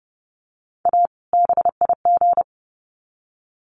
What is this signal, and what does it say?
Morse code, heard as a keyed tone, about 30 wpm, 700 Hz. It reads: A BIG
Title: Sets of 2 Words 30wpm